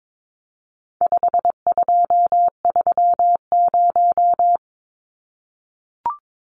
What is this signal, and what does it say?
Morse code, heard as a keyed tone, about 22 wpm, 700 Hz. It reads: 5230
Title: Numbers - 4-Digit 22wpm